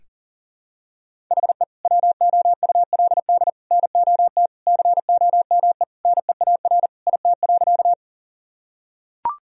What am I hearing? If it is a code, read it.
Morse code, heard as a keyed tone, about 40 wpm, 700 Hz. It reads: HE WOULD NOT COME NEAR IT.
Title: Sentences from Top 1000 Words - Review Part 1 - 40wpm